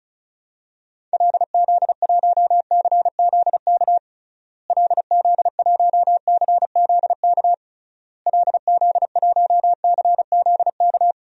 Morse code, heard as a keyed tone, about 36 wpm, 700 Hz. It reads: LZ1CZK LZ1CZK LZ1CZK
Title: DXCC-100 Call Signs - 6 Characters - Repeated 3x - 35wpm